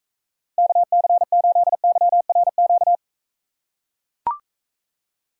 Morse code, heard as a keyed tone, about 42 wpm, 700 Hz. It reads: KC8YRQ